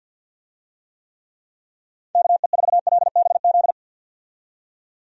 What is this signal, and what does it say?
Morse code, heard as a keyed tone, about 52 wpm, 700 Hz. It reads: KE4LBZ
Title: USA Call Signs - 6 Characters - 50wpm